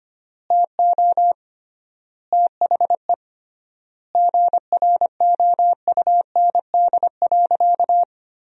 Morse code, heard as a keyed tone, about 25 wpm, 700 Hz. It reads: TO THE GROUND.
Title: 3x Word-Spacing - Sentences from Top 500 Words - 25wpm